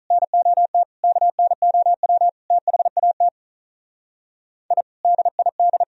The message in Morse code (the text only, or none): NOT KNOW THAT I DID